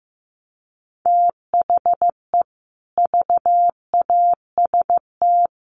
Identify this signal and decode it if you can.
Morse code, heard as a keyed tone, about 15 wpm, 700 Hz. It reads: THE VAST
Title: Sets of 4 Words 15wpm